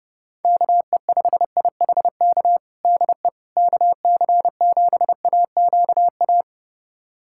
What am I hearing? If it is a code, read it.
Morse code, heard as a keyed tone, about 30 wpm, 700 Hz. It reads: KE5IHK DE KC7AQA
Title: USA Callsign DE Callsign - 6 Characters - 30wpm